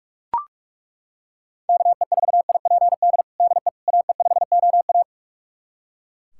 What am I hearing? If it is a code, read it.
Morse code, heard as a keyed tone, about 41 wpm, 690 Hz. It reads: KE4IPD DE AE5OA
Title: USA Callsign DE Callsign 45wpm